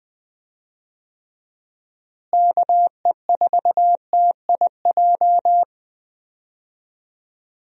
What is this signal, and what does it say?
Morse code, heard as a keyed tone, about 21 wpm, 700 Hz. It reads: KE4TIJ
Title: USA Call Signs - 6 Characters - Repeated 2x - 20wpm